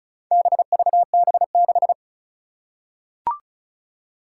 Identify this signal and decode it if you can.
Morse code, heard as a keyed tone, about 36 wpm, 700 Hz. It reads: BVB6
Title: Four Challenging Letter-Number Combinations 35wpm